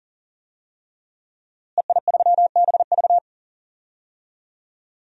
Morse code, heard as a keyed tone, about 40 wpm, 700 Hz. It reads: EI3BV
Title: DXCC-100 Call Signs - 5 Characters - Repeated 3x - 40wpm